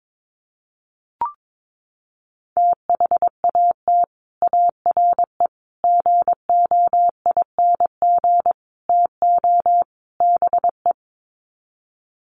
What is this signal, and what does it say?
Morse code, heard as a keyed tone, about 22 wpm, 700 Hz. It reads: THAT ARE GOING TO BE
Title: Sets of 5 Words Encore2 22wpm